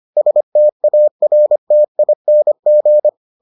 Morse code, heard as a keyed tone, about 25 wpm, 600 Hz. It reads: STARTING